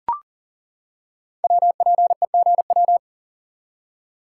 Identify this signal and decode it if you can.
Morse code, heard as a keyed tone, about 39 wpm, 700 Hz. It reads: WPEGW
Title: Code Groups - 5 Letter:Numbers - 40wpm